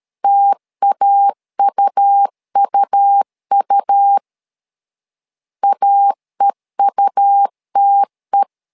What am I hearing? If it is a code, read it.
Morse code, heard as a keyed tone, about 13 wpm, 800 Hz. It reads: TAUUU AEUTE
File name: LEZIONI / 2 UHDN / 2 U 12wpm 1338